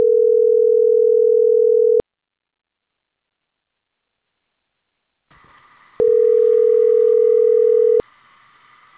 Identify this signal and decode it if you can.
Ringback tone (North America), 440 + 480 Hz.